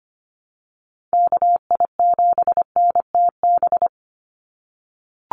Morse code, heard as a keyed tone, about 25 wpm, 700 Hz. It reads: KI7NTB
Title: USA Call Signs 25wpm